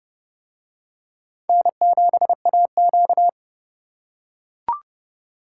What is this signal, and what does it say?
Morse code, heard as a keyed tone, about 29 wpm, 700 Hz. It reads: N7AQ